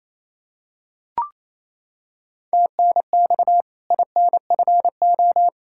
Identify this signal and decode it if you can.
Morse code, heard as a keyed tone, about 27 wpm, 700 Hz. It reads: TNX INFO
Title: K1USN SST QSO Practice 28wpm